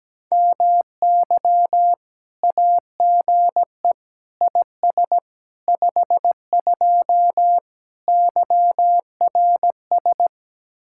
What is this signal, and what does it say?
Morse code, heard as a keyed tone, about 17 wpm, 700 Hz. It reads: MY AGE IS 52 YRS
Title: Speed-Racing Common QSO Phrases - 17wpm